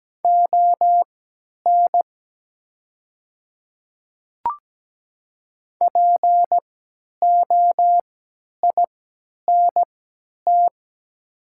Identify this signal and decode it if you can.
Morse code, heard as a keyed tone, about 17 wpm, spaced as for 10 wpm, 700 Hz. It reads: ON POINT
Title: 3x Character Spacing - Top 200 Words - 17wpm